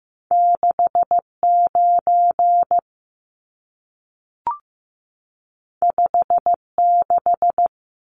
Morse code, heard as a keyed tone, about 15 wpm, 700 Hz. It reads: 69 56
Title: Numbers - 2-Digit 15wpm